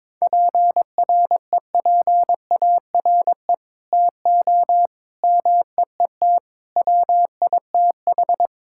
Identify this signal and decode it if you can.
Morse code, heard as a keyed tone, about 21 wpm, 700 Hz. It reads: PREPARE TO MEET WITH
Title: Sentences from Top 1000 Words - Review Part 6 - 22wpm